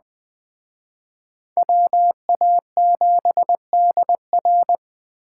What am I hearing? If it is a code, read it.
Morse code, heard as a keyed tone, about 20 wpm, 700 Hz. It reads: WA7DR
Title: Speed Racing - USA 2x2 Call Signs 17wpm